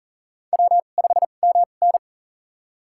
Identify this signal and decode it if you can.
Morse code, heard as a keyed tone, about 40 wpm, spaced as for 29 wpm, 700 Hz. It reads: W5MN
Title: USA Call Signs - 4 Characters - 2x Character Spacing - 40wpm